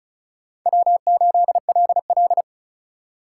Morse code, heard as a keyed tone, about 46 wpm, 700 Hz. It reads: W8LL